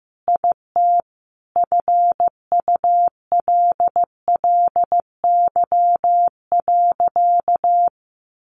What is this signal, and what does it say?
Morse code, heard as a keyed tone, about 15 wpm, 700 Hz. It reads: IT FULLY.
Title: Sentences from Top 1300 Words 15wpm